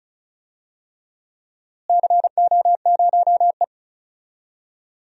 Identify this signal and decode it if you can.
Morse code, heard as a keyed tone, about 36 wpm, 700 Hz. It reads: CO0E
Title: DXCC-100 Call Signs - 4 Characters - Repeated 3x - 35wpm